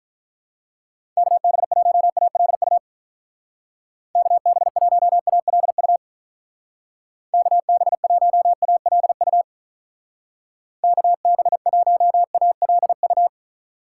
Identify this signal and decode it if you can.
Morse code, heard as a keyed tone, about 47 wpm, 700 Hz. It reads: KB1ALU KB1ALU KB1ALU KB1ALU